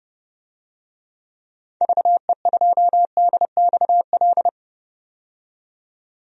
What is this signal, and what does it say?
Morse code, heard as a keyed tone, about 30 wpm, 700 Hz. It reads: VE2DXL
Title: DXCC-100 Call Signs - 6 Characters - 30wpm